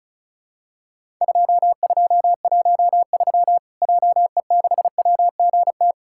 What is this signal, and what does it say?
Morse code, heard as a keyed tone, about 34 wpm, 700 Hz. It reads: 2213 JE6WGT
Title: Worked All Europe DX Contest - QTC Exchange - 35wpm